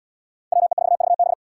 Morse code, heard as a keyed tone, about 78 wpm, 700 Hz. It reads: WE4VB